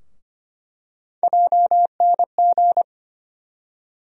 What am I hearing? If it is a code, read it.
Morse code, heard as a keyed tone, about 25 wpm, 700 Hz. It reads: JNG
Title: USA Call Sign Postfixes - 3 Letters 25wpm